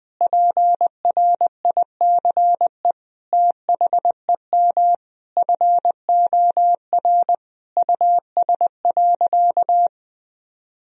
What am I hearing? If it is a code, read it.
Morse code, heard as a keyed tone, about 20 wpm, 700 Hz. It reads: PRICE THEM FOR US.
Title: Sentences from Top 1000 Words - Review Part 4 - 20wpm